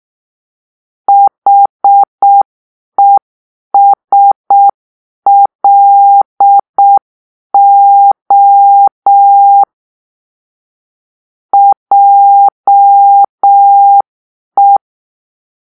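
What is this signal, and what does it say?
Morse code, heard as a keyed tone, about 6 wpm, 800 Hz. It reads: HESLO JE
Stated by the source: Morse-code cipher made by a player team for the Kachna puzzle game.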